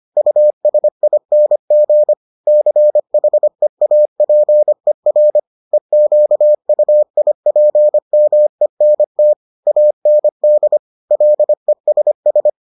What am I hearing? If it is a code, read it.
Morse code, heard as a keyed tone, about 25 wpm, 600 Hz. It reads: USING CHEAPER EQUIPMENT AND LESS